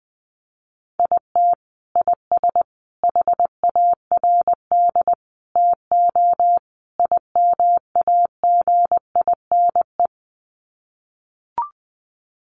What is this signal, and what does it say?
Morse code, heard as a keyed tone, about 20 wpm, 700 Hz. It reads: IT IS HARD TO IMAGINE
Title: Sets of 5 Words Encore2 20wpm